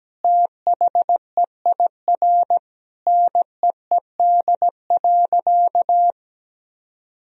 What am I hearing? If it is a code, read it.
Morse code, heard as a keyed tone, about 17 wpm, 700 Hz. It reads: THEIR NEED.
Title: Sentences from Top 1000 Words - Review Part 4 - 17wpm